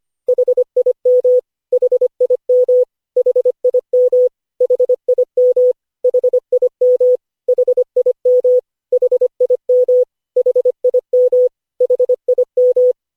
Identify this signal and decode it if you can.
Morse code, heard as a keyed tone, about 25 wpm, 500 Hz. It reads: HIM HIM HIM HIM HIM HIM HIM HIM HIM
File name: cw / Session5 cw training